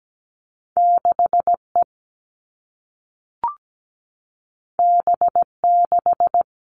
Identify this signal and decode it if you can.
Morse code, heard as a keyed tone, about 17 wpm, 700 Hz. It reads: 6E B6
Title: All Cut Numbers - 2-Digit 17wpm